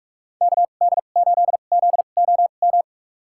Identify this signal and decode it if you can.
Morse code, heard as a keyed tone, about 47 wpm, spaced as for 33 wpm, 700 Hz. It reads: KD8ZOM
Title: USA Call Signs - 6 Characters - 2x Character Spacing - 45wpm